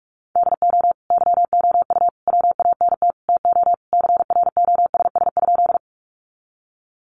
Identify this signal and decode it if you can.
Morse code, heard as a keyed tone, about 46 wpm, 700 Hz. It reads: DO YOU WANT TO CROSS?
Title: Sentences from Top 300 Words 45wpm